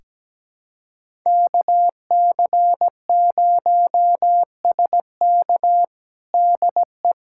Morse code, heard as a keyed tone, about 17 wpm, 700 Hz. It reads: KC0SK DE
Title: USA Callsign DE Callsign 17wpm